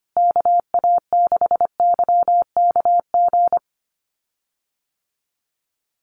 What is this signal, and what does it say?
Morse code, heard as a keyed tone, about 25 wpm, 700 Hz. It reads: KA6YKG
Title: USA Call Signs Repeated 2x 25wpm